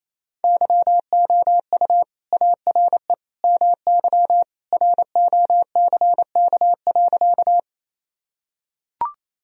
Morse code, heard as a keyed tone, about 28 wpm, 700 Hz. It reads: YOU ARE MY ROCK.
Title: Sentences from Top 1000 Words - Review Part 2 - 28wpm